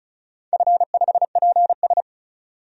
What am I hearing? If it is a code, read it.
Morse code, heard as a keyed tone, about 34 wpm, 700 Hz. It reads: F5PS
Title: DXCC-100 Call Signs - 4 Characters - 35wpm